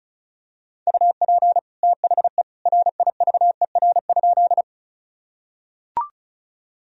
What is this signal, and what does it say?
Morse code, heard as a keyed tone, about 34 wpm, 700 Hz. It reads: UP THE RIVER?
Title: Sentences from Top 1000 Words - Review Part 1 - 35wpm